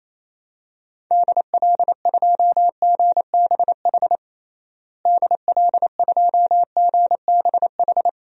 Morse code, heard as a keyed tone, about 28 wpm, 700 Hz. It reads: DL2GBH DL2GBH
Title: DXCC-100 Call Signs - 6 Characters - Repeated 3x - 28wpm